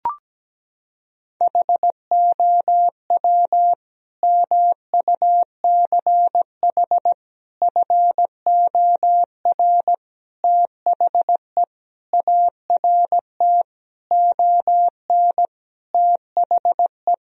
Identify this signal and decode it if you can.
Morse code, heard as a keyed tone, about 17 wpm, 700 Hz. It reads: HOW MUCH FOR THE ART ON THE
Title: Sentences from Top 1000 Words - Review Part 3 - 17wpm